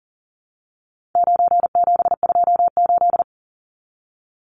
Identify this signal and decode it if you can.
Morse code, heard as a keyed tone, about 40 wpm, 700 Hz. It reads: 9728
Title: Numbers - 4-Digit 40wpm